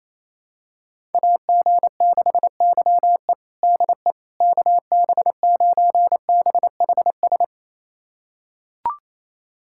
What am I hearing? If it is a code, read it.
Morse code, heard as a keyed tone, about 28 wpm, 700 Hz. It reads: AG6YE DE KB9BHS
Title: USA Callsign DE Callsign 28wpm